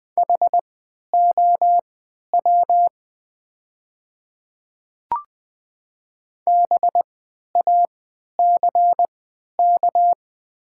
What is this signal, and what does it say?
Morse code, heard as a keyed tone, about 20 wpm, spaced as for 11 wpm, 700 Hz. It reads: HOW BACK
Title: 3x Character Spacing - Top 300 Words - 20wpm